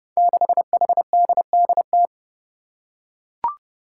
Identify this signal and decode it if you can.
Morse code, heard as a keyed tone, about 30 wpm, 700 Hz. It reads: 6HDDT
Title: Five Challenging Letter-Number Combinations 30wpm